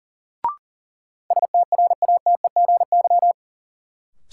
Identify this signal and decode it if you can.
Morse code, heard as a keyed tone, about 40 wpm, 700 Hz. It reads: STRATEGY
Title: Sentences from Top 1000 Words - Review Part 5 - New Word Drill - 40wpm